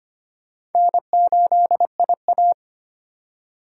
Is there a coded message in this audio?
Yes: Morse code from 0.8 to 2.5 s, heard as a keyed tone, about 32 wpm, 700 Hz. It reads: N8IA